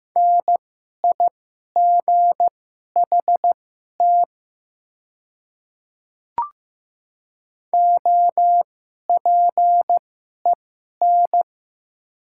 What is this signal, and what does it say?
Morse code, heard as a keyed tone, about 15 wpm, spaced as for 11 wpm, 700 Hz. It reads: NIGHT OPEN
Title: 2x Character Spacing - Top 300 Words - 15wpm